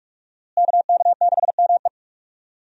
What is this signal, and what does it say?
Morse code, heard as a keyed tone, about 46 wpm, 700 Hz. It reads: KK6ME
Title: USA Call Signs - 5 Characters - Repeated 2x - 45wpm